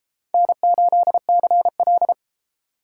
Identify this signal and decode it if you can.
Morse code, heard as a keyed tone, about 29 wpm, 700 Hz. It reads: N8CL